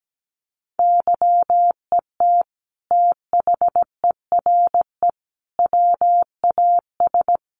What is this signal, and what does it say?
Morse code, heard as a keyed tone, about 17 wpm, 700 Hz. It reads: YET THERE WAS
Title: Sentences from Top 1300 Words 17wpm